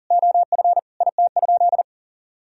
Morse code, heard as a keyed tone, about 39 wpm, 700 Hz. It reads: OF IT?
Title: Sentences from Top 1000 Words - Review Part 3 - 40wpm